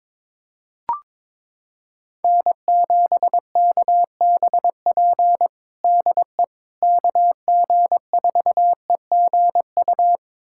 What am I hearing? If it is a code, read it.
Morse code, heard as a keyed tone, about 22 wpm, 700 Hz. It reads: N7KBP DE KG4EGU
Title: USA Callsign DE Callsign 22wpm